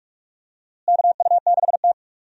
Morse code, heard as a keyed tone, about 46 wpm, 700 Hz. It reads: KU6T
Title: USA Call Signs - 4 Characters - Repeated 2x - 45wpm